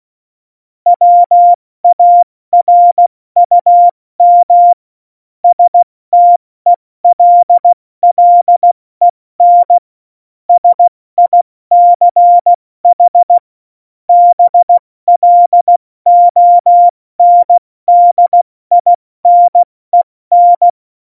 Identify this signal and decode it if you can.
Morse code, heard as a keyed tone, about 16 wpm, 700 Hz. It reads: WARUM STELLEN SICH BLONDINEN
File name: Morsekurs / Witze / Witz 04 Juli 17, Tempo70 F80